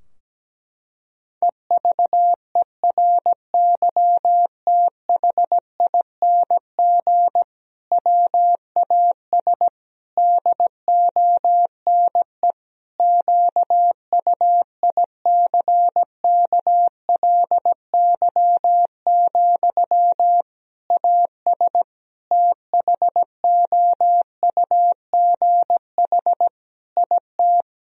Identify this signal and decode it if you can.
Morse code, heard as a keyed tone, about 17 wpm, 700 Hz. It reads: EVERYTHING WAS DONE QUICKLY, AS THOUGH IT
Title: Sentences from Top 1000 Words - Review Part 2 - 17wpm